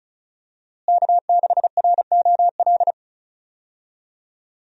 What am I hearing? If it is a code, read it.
Morse code, heard as a keyed tone, about 34 wpm, 700 Hz. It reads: K6ROL
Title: USA Call Signs - 5 Characters - Repeated 2x - 35wpm